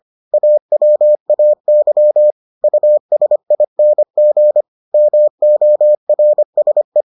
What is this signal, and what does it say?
Morse code, heard as a keyed tone, about 25 wpm, 600 Hz. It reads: AWAY USING MORSE